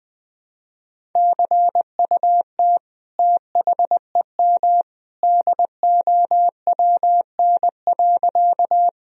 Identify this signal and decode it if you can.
Morse code, heard as a keyed tone, about 20 wpm, 700 Hz. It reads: CUT THEM DOWN.
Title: Sentences from Top 1000 Words - Review Part 1 - 20wpm